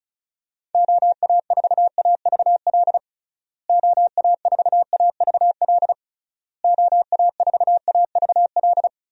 Morse code, heard as a keyed tone, about 34 wpm, 700 Hz. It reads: OA4AVL OA4AVL OA4AVL
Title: USA Call Signs - 6 Characters - Repeated 3x - 35wpm